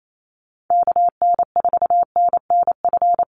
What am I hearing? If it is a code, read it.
Morse code, heard as a keyed tone, about 28 wpm, 700 Hz. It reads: KN4NNF
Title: USA Call Signs 28wpm